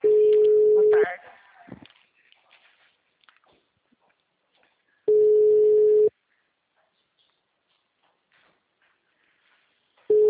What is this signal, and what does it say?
Ringback tone (Europe, 425 Hz), with speech.
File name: out- -2010- - - .150